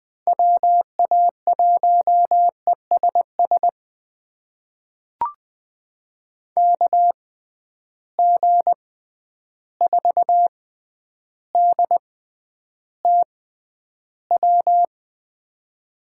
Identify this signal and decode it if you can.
Morse code, heard as a keyed tone, about 20 wpm, spaced as for 10 wpm, 700 Hz. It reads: WA1ESS KG4DTW
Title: USA Call Signs - 6 Characters - 6x Character Spacing - 20wpm